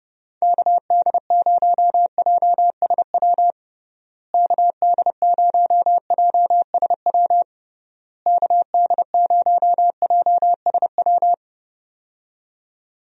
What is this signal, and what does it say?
Morse code, heard as a keyed tone, about 32 wpm, 700 Hz. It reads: KD0JSW KD0JSW KD0JSW
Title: USA Call Signs - 6 Characters - Repeated 3x - 30wpm